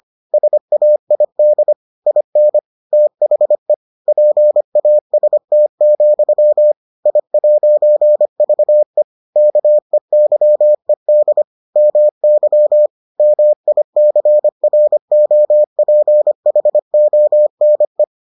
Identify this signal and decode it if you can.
Morse code, heard as a keyed tone, about 25 wpm, 600 Hz. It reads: SAID IN THE PAST, I'VE KEYED MY MICROPHONE